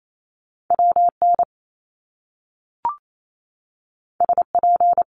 Morse code, heard as a keyed tone, about 28 wpm, 700 Hz. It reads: WN SP
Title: Top Two Letter Combinations in Words 28wpm